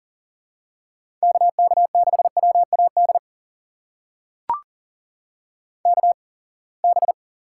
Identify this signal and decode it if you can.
Morse code, heard as a keyed tone, about 39 wpm, spaced as for 27 wpm, 700 Hz. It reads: KK6WAD KB
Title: USA Call Signs - 8x Character Spacing - 40wpm